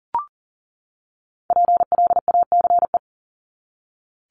Morse code, heard as a keyed tone, about 40 wpm, 700 Hz. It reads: PLACE
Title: Top 500 Words 40wpm